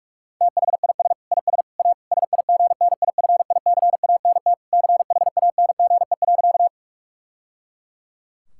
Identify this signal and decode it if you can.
Morse code, heard as a keyed tone, about 46 wpm, 700 Hz. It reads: THIS IS A SIGNIFICANT CHANGE.
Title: Sentences from Top 1000 Words - Review Part 6 - 45wpm